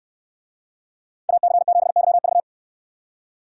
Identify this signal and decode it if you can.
Morse code, heard as a keyed tone, about 69 wpm, 700 Hz. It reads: AC7YV